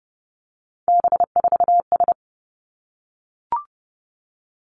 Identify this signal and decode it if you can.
Morse code, heard as a keyed tone, about 30 wpm, 700 Hz. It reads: B4S
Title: Three Challenging Letter-Number Combinations 30wpm